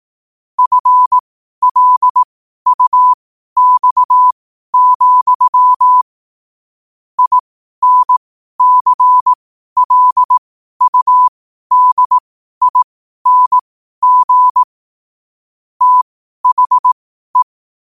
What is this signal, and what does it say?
Morse code, heard as a keyed tone, about 18 wpm, spaced as for 12 wpm, 1000 Hz. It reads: FLUX, INCLUDING THE
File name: qotdfiles / 12wpm